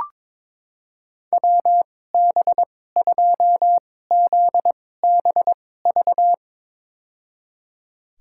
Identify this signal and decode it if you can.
Morse code, heard as a keyed tone, about 22 wpm, spaced as for 16 wpm, 700 Hz. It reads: WB2ZBV
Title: USA Call Signs - 6 Characters - 2x Character Spacing - 22wpm